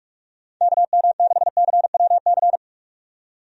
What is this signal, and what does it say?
Morse code, heard as a keyed tone, about 42 wpm, 700 Hz. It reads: KM6CWC
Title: USA Call Signs - 6 Characters - 45wpm